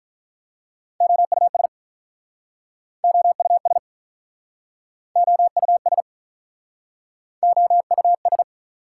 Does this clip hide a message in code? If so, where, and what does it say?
Yes: Morse code from 1.0 to 8.4 s, heard as a keyed tone, about 47 wpm, 690 Hz. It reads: OUS OUS OUS OUS